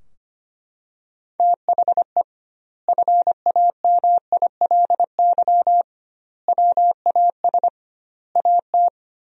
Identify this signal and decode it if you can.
Morse code, heard as a keyed tone, about 25 wpm, 700 Hz. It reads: THE FAMILY WAS AT
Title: 2x Word-Spacing - Sentences from Top 400 Words - 25wpm